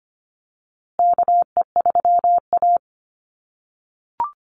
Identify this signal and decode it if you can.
Morse code, heard as a keyed tone, about 25 wpm, 700 Hz. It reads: KE3A
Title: USA Call Signs Repeated 2x 25wpm